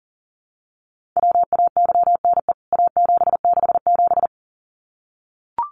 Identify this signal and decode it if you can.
Morse code, heard as a keyed tone, about 40 wpm, 700 Hz. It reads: WAYNE A767
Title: CWT Exchange 40wpm